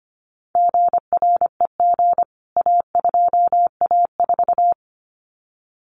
Morse code, heard as a keyed tone, about 25 wpm, 700 Hz. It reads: GREG A2A4
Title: CWT Exchange 25wpm